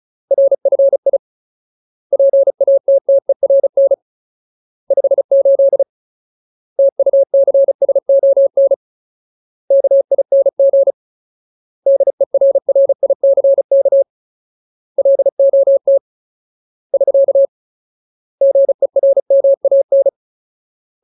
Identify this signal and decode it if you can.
Morse code, heard as a keyed tone, about 34 wpm, 550 Hz. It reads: RFI PATTERN 58 TUCSON KING DERRICK LOT <SK> GERMAN
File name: Ultimate Word List / 35WPM Ultimate